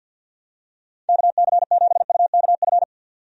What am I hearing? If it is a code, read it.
Morse code, heard as a keyed tone, about 48 wpm, 700 Hz. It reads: KC7UKF
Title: USA Call Signs - 6 Characters - 50wpm